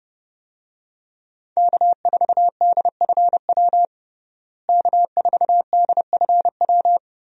Morse code, heard as a keyed tone, about 32 wpm, 700 Hz. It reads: K4DFW K4DFW
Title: USA Call Signs - 5 Characters - Repeated 2x - 30wpm